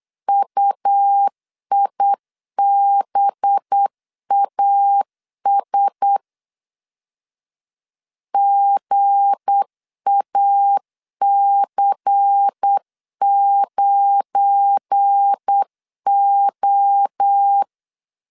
Morse code, heard as a keyed tone, about 8 wpm, 800 Hz. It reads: UIBAS GAC9O
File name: LEZIONI / 7 GF9Z / 7 GF9Z 8wpm 3213